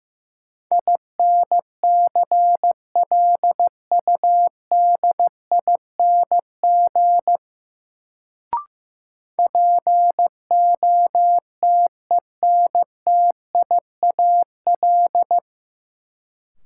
Morse code, heard as a keyed tone, about 16 wpm, 700 Hz. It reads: INCLUDING POTENTIAL
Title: Sentences from Top 1000 Words - Review Part 4 - New Word Drill - 15wpm